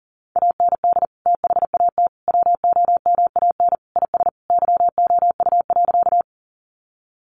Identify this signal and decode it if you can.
Morse code, heard as a keyed tone, about 40 wpm, 700 Hz. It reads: AND THAT WOMAN IS YOU.
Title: Sentences from Top 1100 Words 40wpm